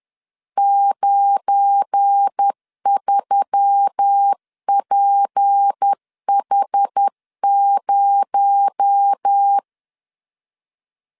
Morse code, heard as a keyed tone, about 10 wpm, 800 Hz. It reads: 93PH0
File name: LEZIONI / 8 YJ45 / 8 YJ4 10wpm 2633